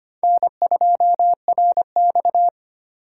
Morse code, heard as a keyed tone, about 31 wpm, 700 Hz. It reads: N2RX